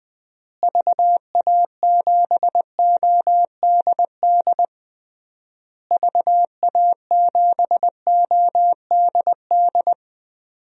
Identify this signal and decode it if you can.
Morse code, heard as a keyed tone, about 20 wpm, 700 Hz. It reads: VA7ODD VA7ODD
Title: DXCC-100 Call Signs - 6 Characters - Repeated 3x - 20wpm